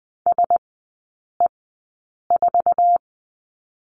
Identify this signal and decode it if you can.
Morse code, heard as a keyed tone, about 20 wpm, spaced as for 8 wpm, 700 Hz. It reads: SE4
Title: 047 Lesson 9e - Number 4 - Three Characters 20wpm